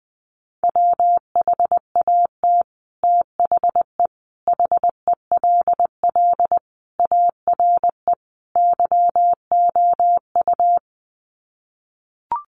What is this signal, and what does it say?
Morse code, heard as a keyed tone, about 20 wpm, 700 Hz. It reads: WHAT THE HELL ARE YOU
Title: Sets of 5 Words Encore 20wpm